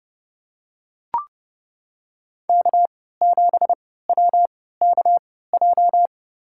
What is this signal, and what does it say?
Morse code, heard as a keyed tone, about 31 wpm, spaced as for 17 wpm, 700 Hz. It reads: K7WKJ
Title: USA Call Signs - 5 Characters - 3x Character Spacing - 30wpm